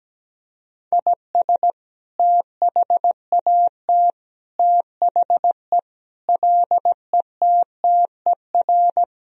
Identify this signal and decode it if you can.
Morse code, heard as a keyed tone, about 17 wpm, 700 Hz. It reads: IS THAT THE LETTER
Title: Sentences from Top 1000 Words - Review Part 2 - 17wpm